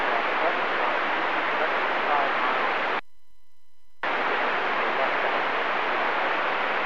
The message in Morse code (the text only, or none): none